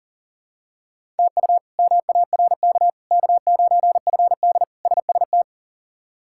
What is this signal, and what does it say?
Morse code, heard as a keyed tone, about 37 wpm, 700 Hz. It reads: TU MARK K9FD SST
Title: K1USN SST QSO Practice - 40wpm